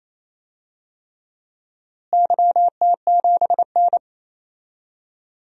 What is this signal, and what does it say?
Morse code, heard as a keyed tone, about 28 wpm, 700 Hz. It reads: YT7N
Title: DXCC-100 Call Signs - 4 Characters - Repeated 3x - 28wpm